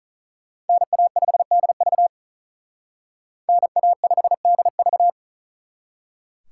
Morse code, heard as a keyed tone, about 46 wpm, 700 Hz. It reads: NA5DV NA5DV